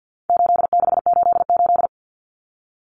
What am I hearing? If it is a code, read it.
Morse code, heard as a keyed tone, about 50 wpm, 700 Hz. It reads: 8688
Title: Numbers - 4-Digit 50wpm